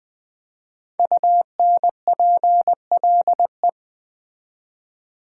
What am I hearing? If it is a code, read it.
Morse code, heard as a keyed tone, about 20 wpm, 700 Hz. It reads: UNPLE
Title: Build A Word - 5 Letters At A Time - 10-Letter Words - 20wpm